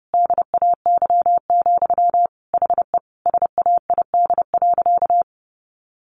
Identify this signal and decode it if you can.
Morse code, heard as a keyed tone, about 30 wpm, 700 Hz. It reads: DAY, HE SAID.
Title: Sentences from Top 900 Words 30wpm